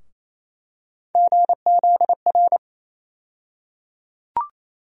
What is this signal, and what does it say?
Morse code, heard as a keyed tone, about 28 wpm, 700 Hz. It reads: GZR
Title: USA Call Sign Postfixes - 3 Letters 28wpm